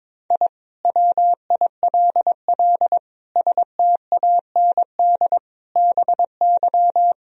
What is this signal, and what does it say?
Morse code, heard as a keyed tone, about 22 wpm, 700 Hz. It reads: I WILL STAND BY
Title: Sentences from Top 1000 Words - Review Part 1 - 22wpm